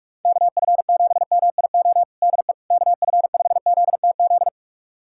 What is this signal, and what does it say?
Morse code, heard as a keyed tone, about 46 wpm, 700 Hz. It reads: KF7MIO DE KF5ZTZ
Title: USA Callsign DE Callsign 45wpm